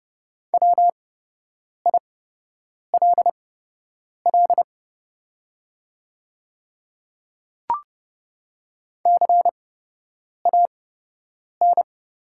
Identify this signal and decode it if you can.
Morse code, heard as a keyed tone, about 29 wpm, spaced as for 8 wpm, 700 Hz. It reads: WILL CAN
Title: 8x Character Spacing - Top 100 Words - 30wpm